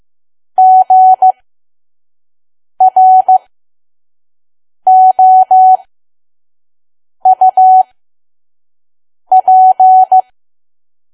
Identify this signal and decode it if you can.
Morse code, heard as a keyed tone, about 15 wpm, spaced as for 5 wpm, 750 Hz. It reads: GROUP
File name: Archive / 5 WPM files / PM